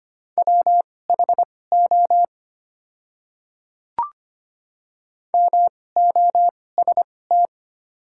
Morse code, heard as a keyed tone, about 25 wpm, spaced as for 18 wpm, 700 Hz. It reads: WHO MOST
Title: 2x Character Spacing - Top 100 Words - 25wpm